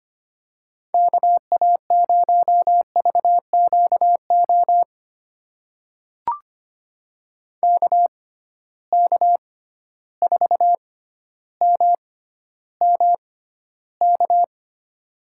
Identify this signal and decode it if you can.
Morse code, heard as a keyed tone, about 25 wpm, spaced as for 13 wpm, 700 Hz. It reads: KA0VQO KK4MMK
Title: USA Call Signs - 6x Character Spacing - 25wpm